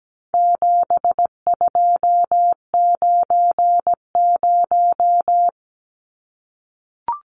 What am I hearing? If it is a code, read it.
Morse code, heard as a keyed tone, about 17 wpm, 700 Hz. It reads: 7290
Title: Numbers - 4-Digit 17wpm